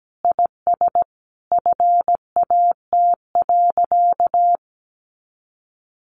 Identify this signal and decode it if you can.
Morse code, heard as a keyed tone, about 17 wpm, 700 Hz. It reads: IS FAT.
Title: Easy Phrases 17wpm